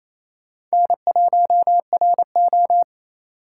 Morse code, heard as a keyed tone, about 36 wpm, 700 Hz. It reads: N1RO